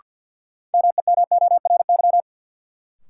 Morse code, heard as a keyed tone, about 53 wpm, 700 Hz. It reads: MEMORY
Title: Sentences from Top 1300 Words - New Word Drill Alt - 50wpm